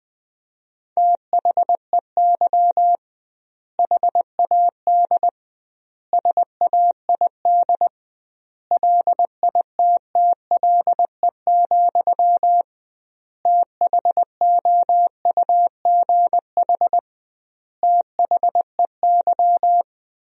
Morse code, heard as a keyed tone, about 20 wpm, 700 Hz. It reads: THEY HAD SAID LITTLE, THOUGH THEY
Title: 2x Word-Spacing - Sentences from Top 400 Words - 20wpm